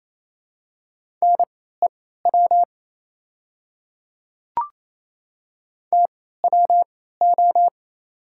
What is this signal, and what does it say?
Morse code, heard as a keyed tone, about 28 wpm, spaced as for 16 wpm, 700 Hz. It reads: NEW TWO
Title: 3x Character Spacing - Top 200 Words - 28wpm